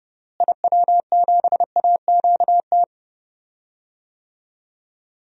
Morse code, heard as a keyed tone, about 29 wpm, 700 Hz. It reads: IW7AQT
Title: DXCC-100 Call Signs - 6 Characters - Repeated 2x - 30wpm